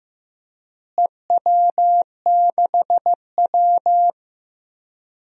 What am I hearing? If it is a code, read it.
Morse code, heard as a keyed tone, about 15 wpm, 700 Hz. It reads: EW6W
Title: DXCC-100 Call Signs - 4 Characters - Repeated 2x - 15wpm